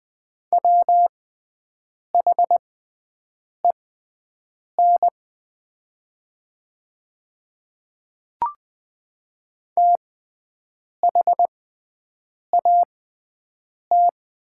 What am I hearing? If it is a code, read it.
Morse code, heard as a keyed tone, about 20 wpm, spaced as for 7 wpm, 700 Hz. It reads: WHEN THAT
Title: 6x Character Spacing - Top 100 Words - 20wpm